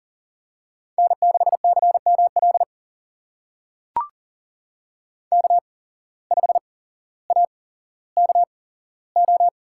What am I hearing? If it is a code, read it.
Morse code, heard as a keyed tone, about 42 wpm, spaced as for 17 wpm, 700 Hz. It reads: N6CML K5AKO
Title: USA Call Signs - 8x Character Spacing - 40wpm